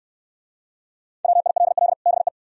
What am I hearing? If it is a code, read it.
Morse code, heard as a keyed tone, about 69 wpm, 690 Hz. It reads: WILL BE